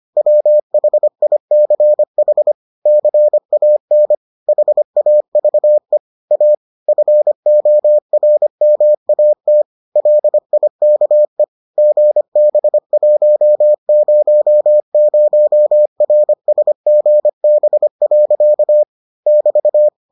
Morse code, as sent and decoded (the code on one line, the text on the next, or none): .-- .... .. -.-. .... / -.-. .- -. / .... .- ...- . / .- / ..-. --- .-. -- .- - / .-.. .. -.- . / --. -... .---- ----- ----- .-. ... --. -... .-.-.- / -...-
WHICH CAN HAVE A FORMAT LIKE GB100RSGB. <BT>